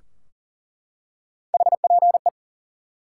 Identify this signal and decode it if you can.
Morse code, heard as a keyed tone, about 38 wpm, 700 Hz. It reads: HPE
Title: Top 90 QSO Elements 40wpm